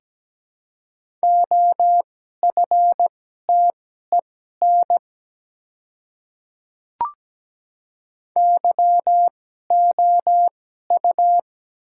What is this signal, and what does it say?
Morse code, heard as a keyed tone, about 17 wpm, spaced as for 12 wpm, 700 Hz. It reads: OFTEN YOU
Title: 2x Character Spacing - Top 300 Words - 17wpm